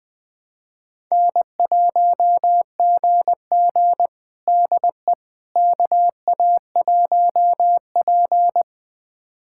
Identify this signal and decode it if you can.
Morse code, heard as a keyed tone, about 20 wpm, 700 Hz. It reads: N1GG DE KA1P
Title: USA Callsign DE Callsign - 4 Characters - 20wpm